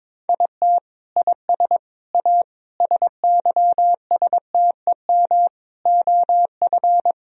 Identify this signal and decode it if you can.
Morse code, heard as a keyed tone, about 22 wpm, 700 Hz. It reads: IT IS A SYSTEM OF
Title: Sentences from Top 1000 Words - Review Part 4 - 22wpm